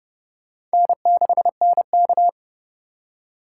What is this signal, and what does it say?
Morse code, heard as a keyed tone, about 31 wpm, 700 Hz. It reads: N6NK
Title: USA Call Signs - 4 Characters - 3x Character Spacing - 30wpm